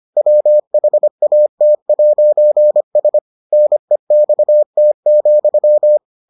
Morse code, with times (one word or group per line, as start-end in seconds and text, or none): 0.2-3.2 WHAT'S
3.5-6.0 NEXT,